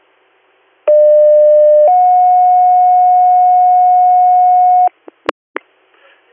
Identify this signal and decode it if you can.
Two-tone page, heard as a one-second tone, then a three-second tone, about 600 then 730 Hz. It is heard from 0.9 to 4.9 s.